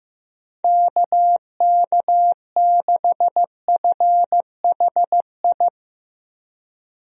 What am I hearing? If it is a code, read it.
Morse code, heard as a keyed tone, about 15 wpm, 700 Hz. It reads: KK6FHI
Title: USA Call Signs - 6 Characters - Repeated 2x - 15wpm